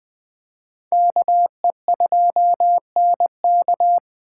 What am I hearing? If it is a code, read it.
Morse code, heard as a keyed tone, about 20 wpm, 700 Hz. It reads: KE2NK